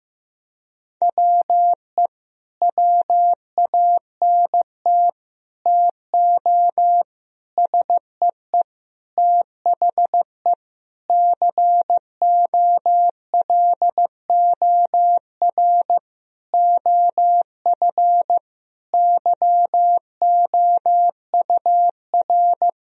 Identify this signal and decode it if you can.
Morse code, heard as a keyed tone, about 15 wpm, 700 Hz. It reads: WE WANT TO SEE THE COLOR OF YOUR
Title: Sentences from Top 1000 Words - Review Part 2 - 15wpm